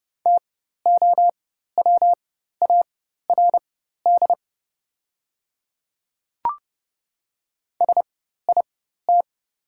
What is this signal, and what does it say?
Morse code, heard as a keyed tone, about 30 wpm, spaced as for 14 wpm, 700 Hz. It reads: TOWARD SIT
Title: 4x Character Spacing - Top 500 Words - 30wpm